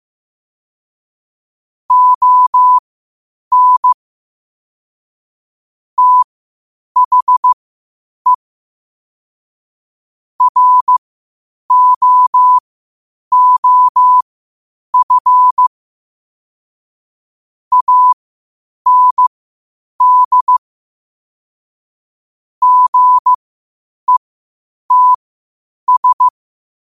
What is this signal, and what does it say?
Morse code, heard as a keyed tone, about 15 wpm, spaced as for 8 wpm, 1000 Hz. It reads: ON THE ROOF AND GETS
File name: qotdfiles / 8wpm